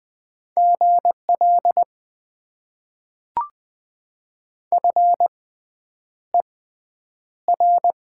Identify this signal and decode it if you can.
Morse code, heard as a keyed tone, about 20 wpm, spaced as for 9 wpm, 700 Hz. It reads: GL FER
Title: Top 90 QSO Elements - 6x Character Spacing - 20wpm